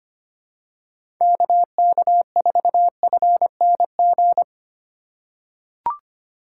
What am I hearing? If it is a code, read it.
Morse code, heard as a keyed tone, about 25 wpm, 700 Hz. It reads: KK4FNG
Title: USA Call Signs - 6 Characters - 4x Character Spacing - 25wpm